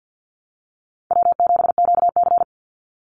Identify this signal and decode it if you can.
Morse code, heard as a keyed tone, about 50 wpm, 700 Hz. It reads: W7QC
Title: Call Signs Repeated 2x - 50wpm